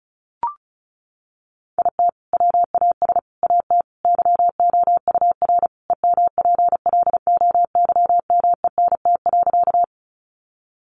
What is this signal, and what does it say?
Morse code, heard as a keyed tone, about 34 wpm, 700 Hz. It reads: IT WAS AT YOUR EMPLOYMENT.
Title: Sentences from Top 1500 Words 35wpm